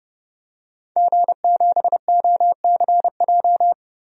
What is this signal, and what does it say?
Morse code, heard as a keyed tone, about 30 wpm, 700 Hz. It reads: G7OCJ
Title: DXCC-100 Call Signs - 5 Characters - Repeated 3x - 30wpm